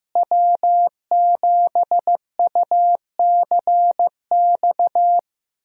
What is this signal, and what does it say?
Morse code, heard as a keyed tone, about 15 wpm, 700 Hz. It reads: W7UCX
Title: USA Call Signs - 5 Characters - Repeated 3x - 15wpm